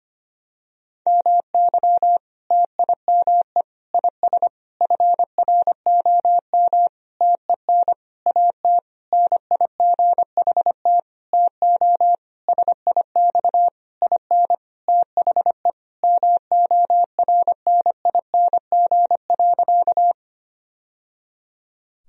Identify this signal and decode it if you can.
Morse code, heard as a keyed tone, about 25 wpm, 700 Hz. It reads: MY TIME IS FROM TEN AT NIGHT TO SIX IN THE MORNING.
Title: Sentences from Top 1000 Words - Review Part 2 - 25wpm